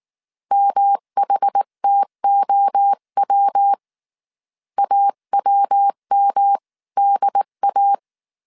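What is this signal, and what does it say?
Morse code, heard as a keyed tone, about 19 wpm, 800 Hz. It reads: MHTOW AWMDA
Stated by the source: Morse code dictation at 18 wpm from the IW3SGT QRS Morse code training course, five-character groups of 300 characters.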